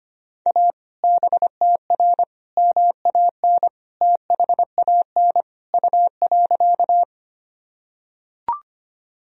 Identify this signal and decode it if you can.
Morse code, heard as a keyed tone, about 25 wpm, 700 Hz. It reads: A BTR MAN THAN U.
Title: Sentences from Top 400 Words and Abbreviations 25wpm